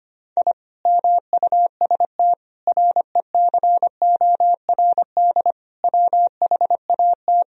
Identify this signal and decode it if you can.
Morse code, heard as a keyed tone, about 25 wpm, 700 Hz. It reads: I MUST RECORD WHAT
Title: Sentences from Top 1000 Words - Review Part 3 - 25wpm